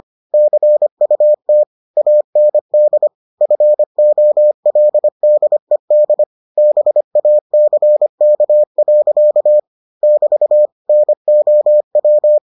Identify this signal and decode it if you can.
Morse code, heard as a keyed tone, about 25 wpm, 600 Hz. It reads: CUT AND FOLDED BACK. <BT> NOW